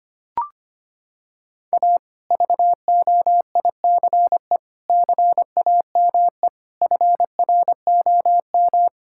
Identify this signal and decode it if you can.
Morse code, heard as a keyed tone, about 25 wpm, 700 Hz. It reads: A VOICE CAME FROM
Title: Sentences from Top 1000 Words - Review Part 5 - 25wpm